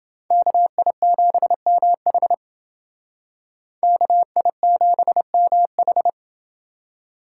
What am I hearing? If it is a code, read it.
Morse code, heard as a keyed tone, about 29 wpm, 700 Hz. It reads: KI7MH KI7MH